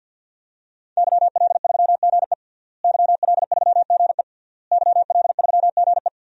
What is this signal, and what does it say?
Morse code, heard as a keyed tone, about 51 wpm, 700 Hz. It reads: YL3GE YL3GE YL3GE
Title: DXCC-100 Call Signs - 5 Characters - Repeated 3x - 50wpm